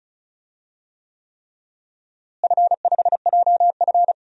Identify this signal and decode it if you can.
Morse code, heard as a keyed tone, about 36 wpm, 700 Hz. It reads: F5JF
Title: DXCC-100 Call Signs - 4 Characters - Repeated 3x - 35wpm